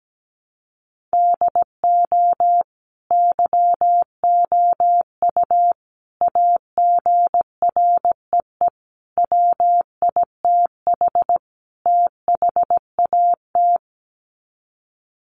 Morse code, heard as a keyed tone, about 17 wpm, 700 Hz. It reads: DO YOU AGREE WITH THAT
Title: Sets of 5 Words Encore 17wpm